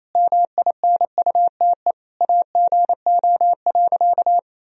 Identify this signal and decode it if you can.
Morse code, heard as a keyed tone, about 28 wpm, 700 Hz. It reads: MINUTE AGO.
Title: Sentences from Top 500 Words and Abbreviations 28wpm